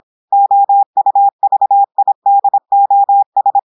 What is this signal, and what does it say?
Morse code, heard as a keyed tone, about 26 wpm, 800 Hz. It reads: OUVIDOS
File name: portugais texte en clair 2 vitesse 26 mots minutes